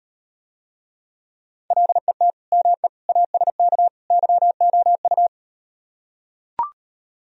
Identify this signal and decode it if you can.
Morse code, heard as a keyed tone, about 38 wpm, 700 Hz. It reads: LET ME ASK YOU